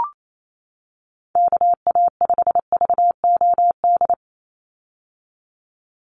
Morse code, heard as a keyed tone, about 28 wpm, 700 Hz. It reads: KA5VOD
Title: USA Call Signs 28wpm